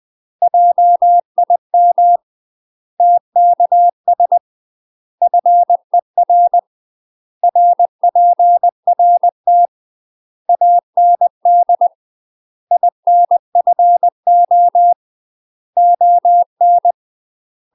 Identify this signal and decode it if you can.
Morse code, heard as a keyed tone, about 20 wpm, 700 Hz. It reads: JIM TKS FER RPRT AND INFO ON